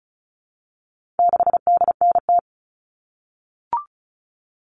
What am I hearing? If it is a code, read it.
Morse code, heard as a keyed tone, about 34 wpm, 700 Hz. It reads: 6DNT
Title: All Cut Numbers - 4-Digit 35wpm